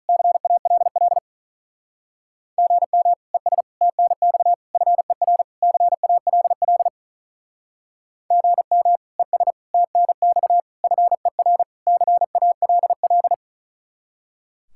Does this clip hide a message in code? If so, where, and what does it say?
Yes: Morse code from 0.1 to 13.3 s, heard as a keyed tone, about 47 wpm, 700 Hz. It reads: CALL GM ES TNX FER CALL GM ES TNX FER CALL